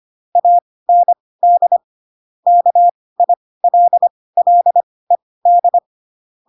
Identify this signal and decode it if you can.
Morse code, heard as a keyed tone, about 25 wpm, spaced as for 18 wpm, 700 Hz. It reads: AND KILLED
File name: Pinocchio / Pinocchio ch7 12 25 ik1hge